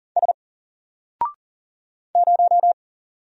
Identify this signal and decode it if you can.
Morse code, heard as a keyed tone, about 40 wpm, 700 Hz. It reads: S 0
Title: Single Letter-Number 40wpm